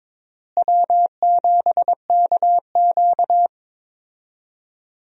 Morse code, heard as a keyed tone, about 23 wpm, 700 Hz. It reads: W7KQ
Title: USA Call Signs - 4 Characters - Repeated 3x - 22wpm